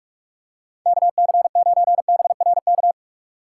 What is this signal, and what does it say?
Morse code, heard as a keyed tone, about 47 wpm, 700 Hz. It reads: KC9BRK
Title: USA Call Signs - 6 Characters - 2x Character Spacing - 45wpm